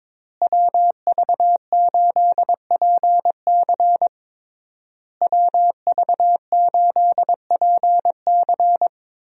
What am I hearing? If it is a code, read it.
Morse code, heard as a keyed tone, about 22 wpm, 700 Hz. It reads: WV8PC WV8PC
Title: USA Call Signs - 5 Characters - Repeated 3x - 22wpm